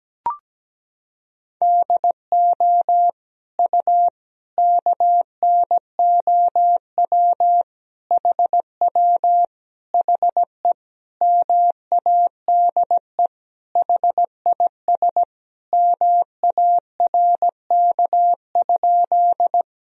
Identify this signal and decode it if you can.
Morse code, heard as a keyed tone, about 17 wpm, 700 Hz. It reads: DO U KNOW HW HE MADE HIS MARK?
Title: Sentences from Top 300 Words and Abbreviations 17wpm